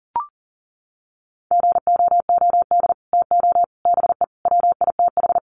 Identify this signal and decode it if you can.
Morse code, heard as a keyed tone, about 40 wpm, 700 Hz. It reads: GOOD TO BE WITH
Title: Sets of 5 Words Encore2 40wpm